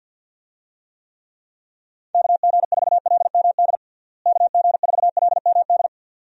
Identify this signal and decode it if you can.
Morse code, heard as a keyed tone, about 48 wpm, 700 Hz. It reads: KG4LMD KG4LMD
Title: USA Call Signs - 6 Characters - Repeated 2x - 50wpm